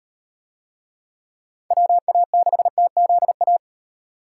Morse code, heard as a keyed tone, about 38 wpm, 700 Hz. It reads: WA6TZA